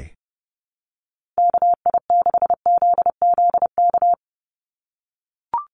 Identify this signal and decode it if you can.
Morse code, heard as a keyed tone, about 30 wpm, 700 Hz. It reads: KI6ZZK
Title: USA Call Signs 30wpm